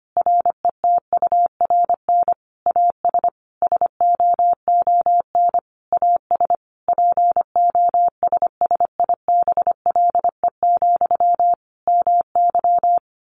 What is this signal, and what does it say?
Morse code, heard as a keyed tone, about 25 wpm, 700 Hz. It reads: RETURN AS SOON AS POSSIBLE, MY
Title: Sentences from Top 800 Words 25wpm